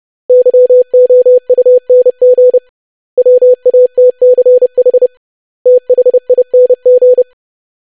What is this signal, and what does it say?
Morse code, heard as a keyed tone, about 30 wpm, 500 Hz. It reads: YOUNG WATCH THING
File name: lesson / web / basic500-25wpm